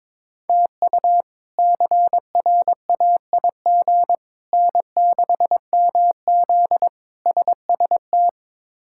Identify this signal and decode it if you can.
Morse code, heard as a keyed tone, about 22 wpm, 700 Hz. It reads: TU CRAIG N6MZ SST
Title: K1USN SST QSO Practice - 22wpm